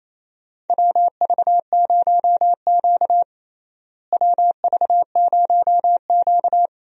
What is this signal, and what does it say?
Morse code, heard as a keyed tone, about 28 wpm, 700 Hz. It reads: WV0Q WV0Q
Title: USA Call Signs - 4 Characters - Repeated 3x - 28wpm